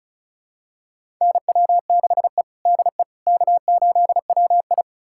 Morse code, heard as a keyed tone, about 36 wpm, 700 Hz. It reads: NW6E DE K8WI
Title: USA Callsign DE Callsign - 4 Characters - 35wpm